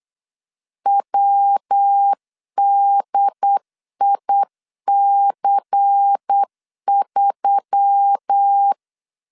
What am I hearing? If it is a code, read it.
Morse code, heard as a keyed tone, about 8 wpm, 800 Hz. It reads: WDIC3
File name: LEZIONI / 6 PKX0 / 6 P 8wpm 3025